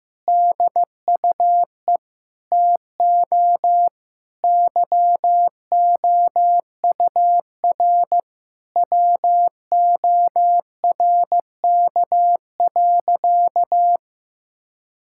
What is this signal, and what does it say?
Morse code, heard as a keyed tone, about 15 wpm, 700 Hz. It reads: DUE TO YOUR WORK.
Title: Sentences from Top 1000 Words - Review Part 6 - 15wpm